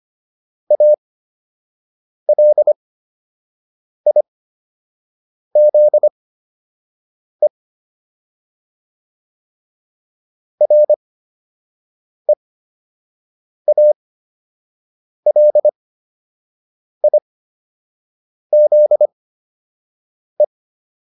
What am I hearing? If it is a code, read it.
Morse code, heard as a keyed tone, about 25 wpm, spaced as for 6 wpm, 600 Hz. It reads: ALIZE REALIZE